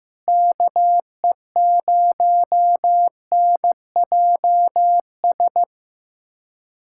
Morse code, heard as a keyed tone, about 16 wpm, 700 Hz. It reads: KE0NJS
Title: USA Callsign DE Callsign 15wpm